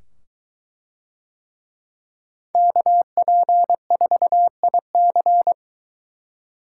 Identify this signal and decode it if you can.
Morse code, heard as a keyed tone, about 23 wpm, 700 Hz. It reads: KP4IC